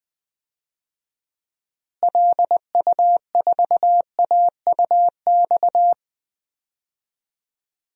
Morse code, heard as a keyed tone, about 20 wpm, 700 Hz. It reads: LU4AUX
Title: DXCC-100 Call Signs - 6 Characters - Repeated 2x - 20wpm